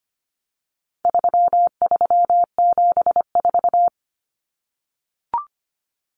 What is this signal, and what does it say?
Morse code, heard as a keyed tone, about 25 wpm, 700 Hz. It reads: 3374
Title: Numbers - 4-Digit 25wpm